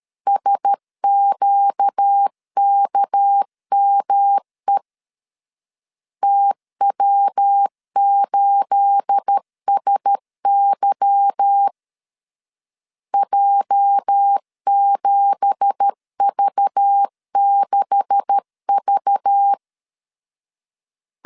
Morse code, heard as a keyed tone, about 13 wpm, 800 Hz. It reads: SQKME TW8SY J7V6V
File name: CW / DET / 1 9 1500C S A 12wpm 3800